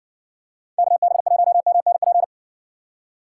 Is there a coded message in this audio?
Yes: Morse code from 0.8 to 2.2 s, heard as a keyed tone, about 60 wpm, 700 Hz. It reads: KB1MNP